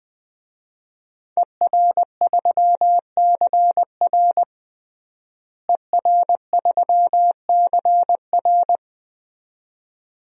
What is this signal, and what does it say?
Morse code, heard as a keyed tone, about 21 wpm, 700 Hz. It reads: ER3CR ER3CR
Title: DXCC-100 Call Signs - 5 Characters - Repeated 2x - 20wpm